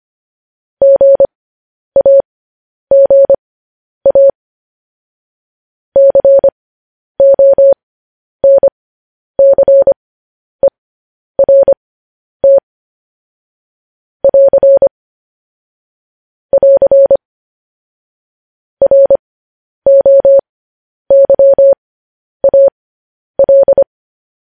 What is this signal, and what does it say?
Morse code, heard as a keyed tone, about 25 wpm, spaced as for 10 wpm, 550 Hz. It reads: GAGA CONCERT <AR> <AR> ROYAL
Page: news-headlines-in-morse-code-at-10-wpm